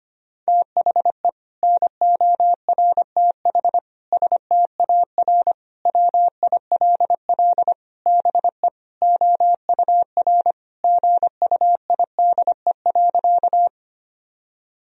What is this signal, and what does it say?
Morse code, heard as a keyed tone, about 25 wpm, 700 Hz. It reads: THE NORTH STAR WILL BE OUR GUIDE.
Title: Sentences from Top 1000 Words - Review Part 3 - 25wpm